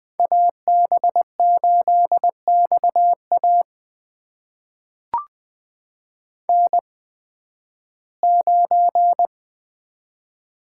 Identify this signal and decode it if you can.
Morse code, heard as a keyed tone, about 20 wpm, spaced as for 13 wpm, 700 Hz. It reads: AB8XA N9
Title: USA Call Signs - 8x Character Spacing - 20wpm